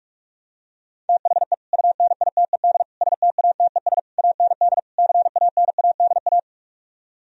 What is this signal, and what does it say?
Morse code, heard as a keyed tone, about 48 wpm, 700 Hz. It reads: THE UNITED STATES AND CANADA